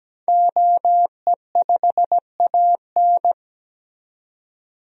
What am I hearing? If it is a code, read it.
Morse code, heard as a keyed tone, about 17 wpm, 700 Hz. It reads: OE5AN
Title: DXCC-100 Call Signs - 5 Characters - 17wpm